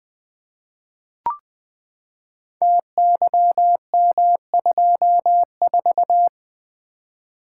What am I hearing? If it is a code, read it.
Morse code, heard as a keyed tone, about 20 wpm, 700 Hz. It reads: TYM24
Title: Code Groups - 5 Letter:Numbers - 20wpm